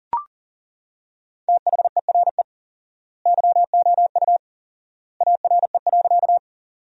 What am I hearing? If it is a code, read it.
Morse code, heard as a keyed tone, about 40 wpm, 700 Hz. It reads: THERE YOU ARE.
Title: 4x Word-Spacing - Sentences from Top 100 Words - 40wpm